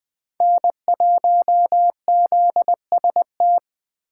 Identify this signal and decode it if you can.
Morse code, heard as a keyed tone, about 20 wpm, 700 Hz. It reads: N1ZST